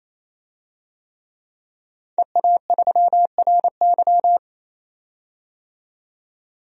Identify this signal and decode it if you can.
Morse code, heard as a keyed tone, about 28 wpm, 700 Hz. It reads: EA3RY
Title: DXCC-100 Call Signs - 5 Characters - 28wpm